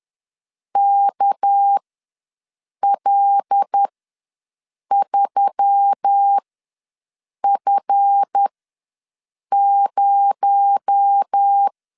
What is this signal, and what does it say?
Morse code, heard as a keyed tone, about 10 wpm, spaced as for 6 wpm, 800 Hz. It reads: KL3F0
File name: LEZIONI / 7 GF9Z / 7 GF9 ws 10wpm 4034